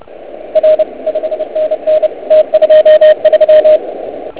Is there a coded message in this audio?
Yes: Morse code from 0.6 to 3.8 s, heard as a keyed tone, about 30 wpm, 620 Hz, with radio noise. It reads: R 5NN T23